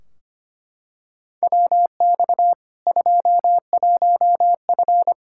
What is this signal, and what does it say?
Morse code, heard as a keyed tone, about 25 wpm, 700 Hz. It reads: WX 21F
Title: K1USN SST QSO Practice 25wpm